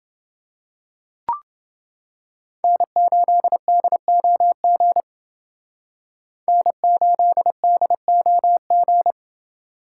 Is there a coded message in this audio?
Yes: Morse code from 2.6 to 9.1 s, heard as a keyed tone, about 29 wpm, 700 Hz. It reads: N8DOG N8DOG